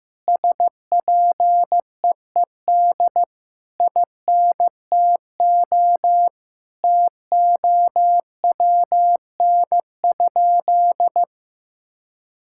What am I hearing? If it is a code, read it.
Morse code, heard as a keyed tone, about 15 wpm, 700 Hz. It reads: SPEED INTO TOWN?
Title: Sentences from Top 1000 Words - Review Part 5 - 15wpm